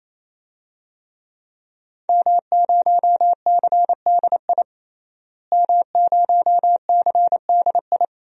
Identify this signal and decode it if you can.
Morse code, heard as a keyed tone, about 28 wpm, 700 Hz. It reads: M0CDI M0CDI
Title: DXCC-100 Call Signs - 5 Characters - Repeated 3x - 28wpm